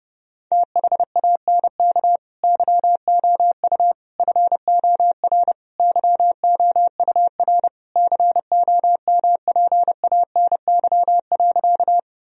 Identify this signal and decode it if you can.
Morse code, heard as a keyed tone, about 30 wpm, 700 Hz. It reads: THANK YOU FOR YOUR COMPANY.
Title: Sentences from Top 1000 Words - Review Part 3 - 30wpm